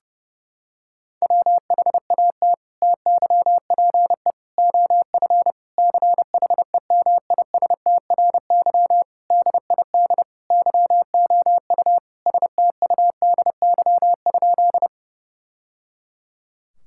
Morse code, heard as a keyed tone, about 30 wpm, 700 Hz. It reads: WHAT TYPE OF CHEMISTRY DID YOU STUDY?
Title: Sentences from Top 1000 Words - Review Part 6 - 30wpm